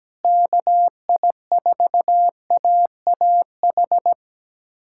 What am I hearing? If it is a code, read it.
Morse code, heard as a keyed tone, about 17 wpm, 700 Hz. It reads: KI4AAH
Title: USA Callsign DE Callsign - 6 Characters - 17wpm